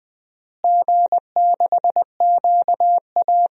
Morse code, heard as a keyed tone, about 20 wpm, 700 Hz. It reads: G6QA
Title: DXCC-100 Call Signs - 4 Characters - Repeated 3x - 20wpm